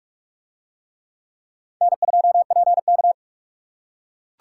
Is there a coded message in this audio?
Yes: Morse code from 1.8 to 3.1 s, heard as a keyed tone, about 44 wpm, 700 Hz. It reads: N2PK